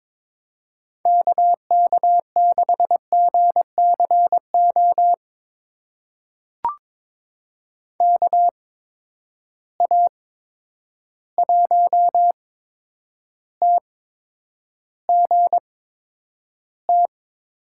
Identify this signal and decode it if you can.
Morse code, heard as a keyed tone, about 22 wpm, spaced as for 9 wpm, 700 Hz. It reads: KK6GCO KA1TGT
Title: USA Call Signs - 8x Character Spacing - 22wpm